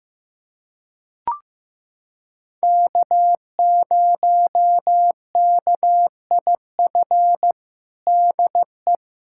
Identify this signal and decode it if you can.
Morse code, heard as a keyed tone, about 16 wpm, 700 Hz. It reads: K0KIF DE
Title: USA Callsign DE Callsign - 5 Characters - 15wpm